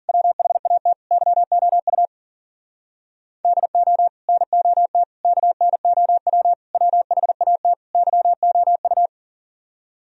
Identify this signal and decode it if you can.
Morse code, heard as a keyed tone, about 53 wpm, 700 Hz. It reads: WHAT YOU DO NOT KNOW WHAT YOU